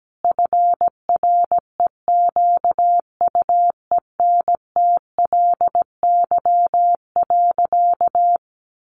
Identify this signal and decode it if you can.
Morse code, heard as a keyed tone, about 17 wpm, 700 Hz. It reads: FREQUENTLY.
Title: Sentences from Top 1300 Words 17wpm